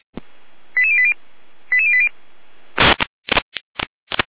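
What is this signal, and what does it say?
Five-tone selective call (ZVEI), heard as quick runs of five tones, tones of about 70 ms. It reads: 88080 88080
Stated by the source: ZVEI five-tone selective call received by SDR.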